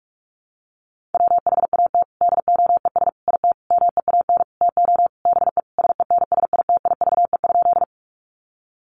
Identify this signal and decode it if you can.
Morse code, heard as a keyed tone, about 46 wpm, 700 Hz. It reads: WHAT DOES IT MEAN TO BE SENSITIVE?
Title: Sentences from Top 1400 Words 45wpm